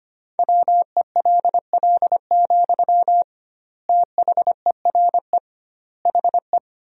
Morse code, heard as a keyed tone, about 25 wpm, 700 Hz. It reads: WELL, THERE HE
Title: 2x Word-Spacing - Sentences from Top 500 Words - 25wpm